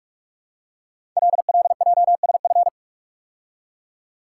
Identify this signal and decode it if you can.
Morse code, heard as a keyed tone, about 44 wpm, 700 Hz. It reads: LLJSF
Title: Code Groups - 5 Letters - 45wpm